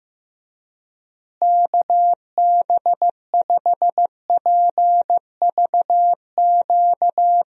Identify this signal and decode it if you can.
Morse code, heard as a keyed tone, about 15 wpm, 700 Hz. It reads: KB5PVQ
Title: USA Call Signs - 6 Characters - Repeated 3x - 15wpm